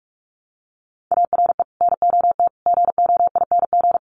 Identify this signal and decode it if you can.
Morse code, heard as a keyed tone, about 44 wpm, 700 Hz. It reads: ARE NOT GOING
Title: Sets of 5 Words Encore 45wpm